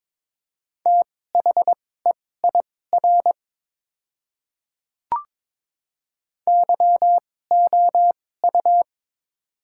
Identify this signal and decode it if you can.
Morse code, heard as a keyed tone, about 21 wpm, spaced as for 16 wpm, 700 Hz. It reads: THEIR YOU
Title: 2x Character Spacing - Top 100 Words - 22wpm